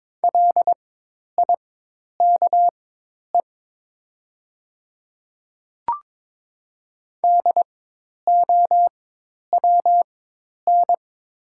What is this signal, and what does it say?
Morse code, heard as a keyed tone, about 22 wpm, spaced as for 10 wpm, 700 Hz. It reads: LIKE DOWN
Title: 4x Character Spacing - Top 100 Words - 22wpm